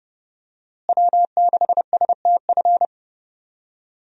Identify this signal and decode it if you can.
Morse code, heard as a keyed tone, about 30 wpm, 700 Hz. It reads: W6STF
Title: USA Call Signs - 5 Characters - Repeated 2x - 30wpm